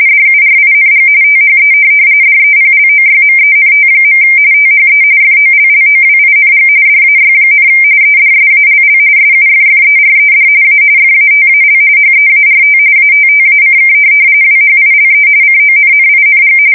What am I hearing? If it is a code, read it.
RTTY, about 45.45 baud. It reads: DURING THE DEPLOYMENT. THE FOLLOWING DAY, EDSALL SUFFERED A FATAL HEART ATTACK WHILE CONNING
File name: fsk170 / USS-Missouri